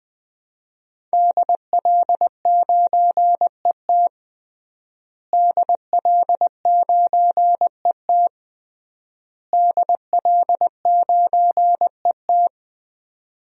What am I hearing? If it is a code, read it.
Morse code, heard as a keyed tone, about 21 wpm, 700 Hz. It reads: DL9ET DL9ET DL9ET
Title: DXCC-100 Call Signs - 5 Characters - Repeated 3x - 20wpm